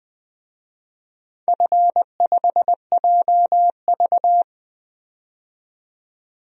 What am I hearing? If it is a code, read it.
Morse code, heard as a keyed tone, about 20 wpm, 700 Hz. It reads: F5JV
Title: DXCC-100 Call Signs - 4 Characters - 20wpm